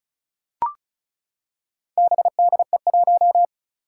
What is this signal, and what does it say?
Morse code, heard as a keyed tone, about 36 wpm, 700 Hz. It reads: BDE1
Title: Four Challenging Letter-Number Combinations 35wpm